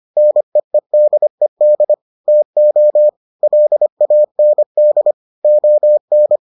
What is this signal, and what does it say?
Morse code, heard as a keyed tone, about 25 wpm, 600 Hz. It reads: NEEDED TO LAND ON